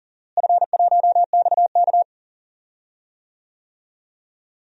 Morse code, heard as a keyed tone, about 40 wpm, 700 Hz. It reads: F1XK
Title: DXCC-100 Call Signs - 4 Characters - Repeated 2x - 40wpm